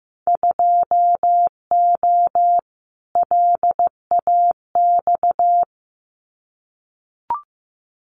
Morse code, heard as a keyed tone, about 15 wpm, 700 Hz. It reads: 2O LAX
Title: Winter Field Day Exchanges 15wpm